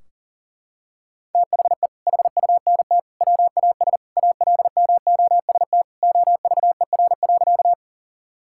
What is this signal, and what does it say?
Morse code, heard as a keyed tone, about 40 wpm, 700 Hz. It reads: THE HUNT WAS ALMOST OVER.
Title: Sentences from Top 1600 Words 40wpm